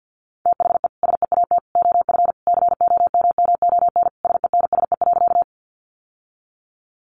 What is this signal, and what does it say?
Morse code, heard as a keyed tone, about 50 wpm, 700 Hz. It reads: THE SEAT OF COMMON SENSE.
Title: Sentences from Top 1500 Words 50wpm